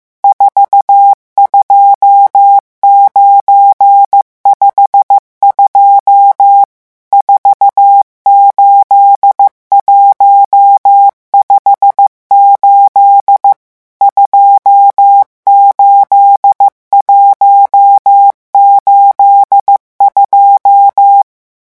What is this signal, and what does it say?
Morse code, heard as a keyed tone, about 15 wpm, 800 Hz. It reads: 42952 48158 28182
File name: morse mp3 / revision11 chiffres vitesse 14